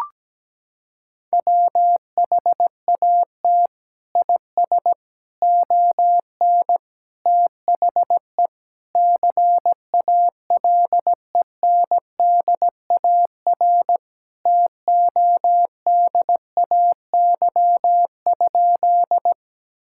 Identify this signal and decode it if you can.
Morse code, heard as a keyed tone, about 17 wpm, 700 Hz. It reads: WHAT IS ON THE CALENDAR TODAY?
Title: Sentences from Top 1600 Words 17wpm